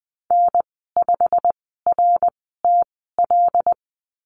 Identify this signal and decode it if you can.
Morse code, heard as a keyed tone, about 20 wpm, spaced as for 15 wpm, 700 Hz. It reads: N5RTL
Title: 088 Lesson 15g - Number 5 - Call Signs 20wpm